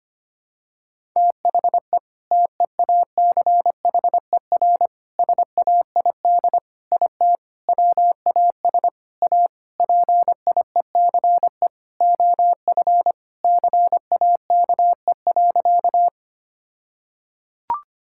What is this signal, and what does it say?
Morse code, heard as a keyed tone, about 25 wpm, 700 Hz. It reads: THE TEACHER SAID IT WAS A PIECE OF CAKE.
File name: practice / 25wpm / All-Age Appropriate Jokes - 25wpm